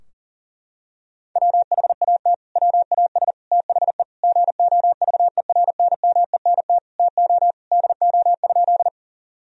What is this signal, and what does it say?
Morse code, heard as a keyed tone, about 40 wpm, 700 Hz. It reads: WHAT WAS THE GOVERNMENT TO DO?
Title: Sentences from Top 1000 Words - Review Part 3 - 40wpm